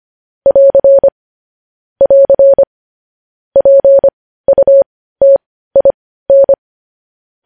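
Morse code, heard as a keyed tone, about 25 wpm, spaced as for 15 wpm, 550 Hz. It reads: <AR> <AR> PUTIN
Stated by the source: Morse code news headlines at 15 wpm.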